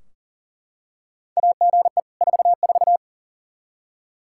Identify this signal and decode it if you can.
Morse code, heard as a keyed tone, about 40 wpm, 700 Hz. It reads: AGE 44
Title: Practice QSO - Exchange 3 - More Info - 40wpm